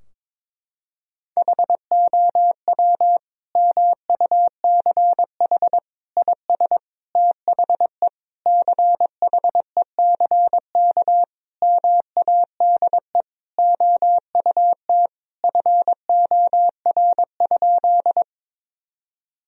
Morse code, heard as a keyed tone, about 22 wpm, 700 Hz. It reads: HOW MUCH IS THE CHECK MADE OUT FOR?
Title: Sentences from Top 1000 Words - Review Part 3 - 22wpm